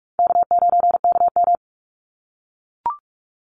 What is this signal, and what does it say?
Morse code, heard as a keyed tone, about 44 wpm, 700 Hz. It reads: K9KM
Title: Call Signs - 45wpm